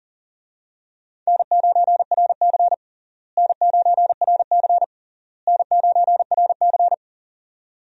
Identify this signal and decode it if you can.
Morse code, heard as a keyed tone, about 40 wpm, 700 Hz. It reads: N9RC N9RC N9RC
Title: USA Call Signs - 4 Characters - Repeated 3x - 40wpm